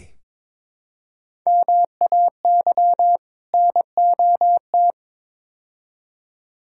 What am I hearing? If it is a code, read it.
Morse code, heard as a keyed tone, about 22 wpm, 700 Hz. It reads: MAY NOT
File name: practice / 22wpm / Build A Phrase - 5-Word Phrases - 22wpm